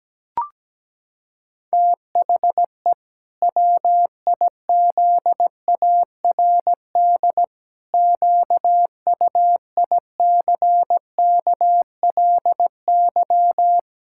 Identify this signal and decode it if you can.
Morse code, heard as a keyed tone, about 17 wpm, 700 Hz. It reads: THE WIZARD QUICKLY
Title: Pangram Challenge v2 - 17wpm